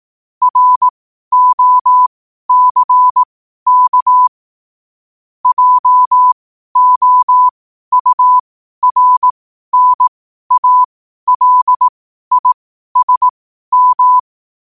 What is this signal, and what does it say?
Morse code, heard as a keyed tone, about 18 wpm, spaced as for 12 wpm, 1000 Hz. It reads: ROCK JOURNALISM
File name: qotdfiles / 12wpm